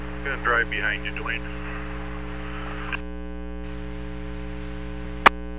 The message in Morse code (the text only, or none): none